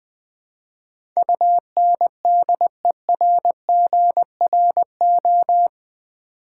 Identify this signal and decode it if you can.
Morse code, heard as a keyed tone, about 20 wpm, 700 Hz. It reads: UNDERGRO
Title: Build A Word - 4 Letters At A Time - 11-Letter Words - 20wpm